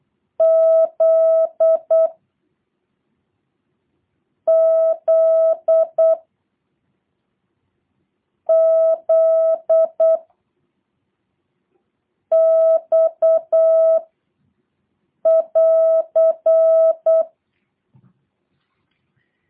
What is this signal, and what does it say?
Morse code, heard as a keyed tone, about 8 wpm, 650 Hz. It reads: Z Z Z X <AR>